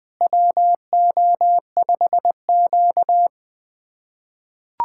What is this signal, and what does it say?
Morse code, heard as a keyed tone, about 20 wpm, 700 Hz. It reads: WO5Q
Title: USA Callsign DE Callsign - 4 Characters - 20wpm